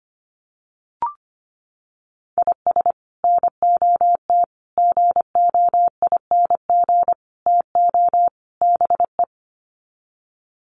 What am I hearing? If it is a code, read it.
Morse code, heard as a keyed tone, about 25 wpm, 700 Hz. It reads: IS NOT GOING TO BE
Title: Sets of 5 Words 25wpm